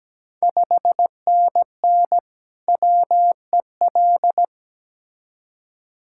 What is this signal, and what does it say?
Morse code, heard as a keyed tone, about 17 wpm, 700 Hz. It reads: 5NN WEL
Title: Ontario QSO Party Exchange 17wpm